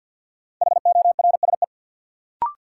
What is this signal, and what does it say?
Morse code, heard as a keyed tone, about 49 wpm, 700 Hz. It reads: HORSE
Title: Sentences from Top 300 Words - New Word Drill - 50wpm